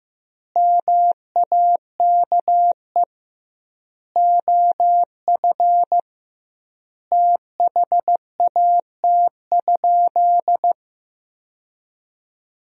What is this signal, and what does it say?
Morse code, heard as a keyed tone, about 15 wpm, 700 Hz. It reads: MAKE OF THAT?
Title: 2x Word-Spacing - Sentences from Top 100 Words - 15wpm